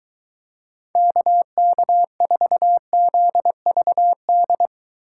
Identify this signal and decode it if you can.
Morse code, heard as a keyed tone, about 23 wpm, 700 Hz. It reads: KK4ZVD